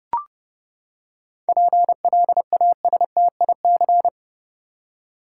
Morse code, heard as a keyed tone, about 30 wpm, 700 Hz. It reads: PLASTIC
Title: Seven Letter Words - 30wpm